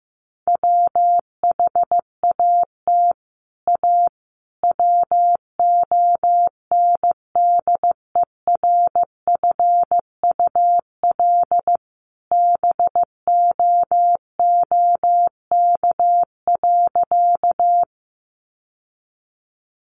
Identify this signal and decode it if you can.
Morse code, heard as a keyed tone, about 15 wpm, 700 Hz. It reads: WHAT A WONDERFUL BOOK.
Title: Sentences from Top 1100 Words 15wpm